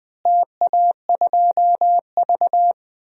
Morse code, heard as a keyed tone, about 20 wpm, 700 Hz. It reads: TA2V
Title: DXCC-100 Call Signs - 4 Characters - Repeated 2x - 20wpm